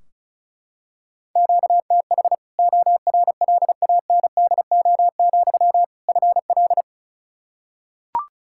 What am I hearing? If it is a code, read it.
Morse code, heard as a keyed tone, about 36 wpm, 700 Hz. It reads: QTH ORLANDO, FL
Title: Practice QSO - Exchange 1 - Beginning - 35wpm